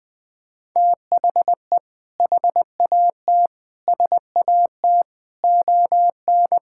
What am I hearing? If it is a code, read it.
Morse code, heard as a keyed tone, about 21 wpm, 700 Hz. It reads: THE HAT SAT ON
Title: Sentences with 2-Word Phrases - Spoken Beginning - 20wpm